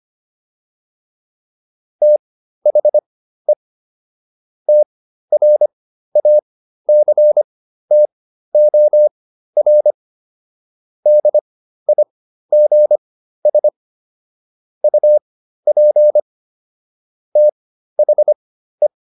Morse code, heard as a keyed tone, about 25 wpm, spaced as for 13 wpm, 600 Hz. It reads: THE TRACTOR DIGS UP THE